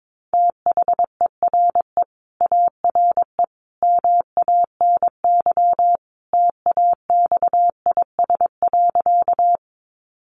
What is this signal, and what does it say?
Morse code, heard as a keyed tone, about 22 wpm, 700 Hz. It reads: THERE ARE MANY TAXIS.
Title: Easy Phrases 22wpm